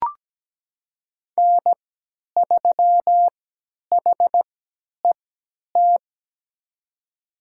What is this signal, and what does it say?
Morse code, heard as a keyed tone, about 17 wpm, spaced as for 10 wpm, 700 Hz. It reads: N3HET
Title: USA Call Signs - 5 Characters - 3x Character Spacing - 17wpm